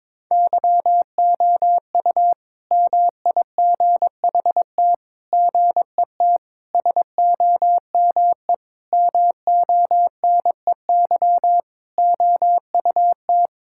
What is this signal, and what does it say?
Morse code, heard as a keyed tone, about 21 wpm, 700 Hz. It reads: YOU MIGHT GET SOME MONEY OUT
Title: Sentences from Top 1000 Words - Review Part 5 - 22wpm